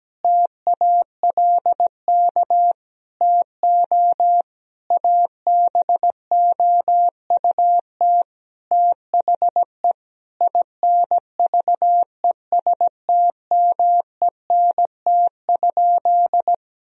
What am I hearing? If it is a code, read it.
Morse code, heard as a keyed tone, about 17 wpm, 700 Hz. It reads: TALK TO ABOUT THE INVESTMENT?
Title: Sentences from Top 1000 Words - Review Part 4 - 17wpm